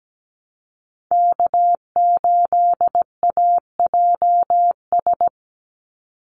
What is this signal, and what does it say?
Morse code, heard as a keyed tone, about 17 wpm, 700 Hz. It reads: K8AJS
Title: Call Signs - 17wpm